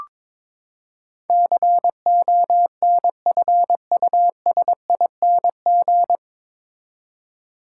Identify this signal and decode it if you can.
Morse code, heard as a keyed tone, about 22 wpm, 700 Hz. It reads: CONFUSING
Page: Morse Code Ninja practice 22wpm